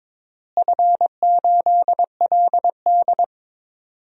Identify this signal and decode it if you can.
Morse code, heard as a keyed tone, about 22 wpm, 700 Hz. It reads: F8LD
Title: DXCC-100 Call Signs - 4 Characters - Repeated 2x - 22wpm